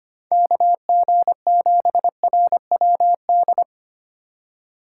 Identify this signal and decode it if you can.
Morse code, heard as a keyed tone, about 25 wpm, 700 Hz. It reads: KG7RWD
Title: USA Call Signs - 2x Character Spacing - 25wpm